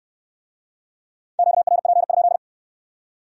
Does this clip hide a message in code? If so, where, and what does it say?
Yes: Morse code from 1.4 to 2.4 s, heard as a keyed tone, about 62 wpm, 700 Hz. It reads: QRP?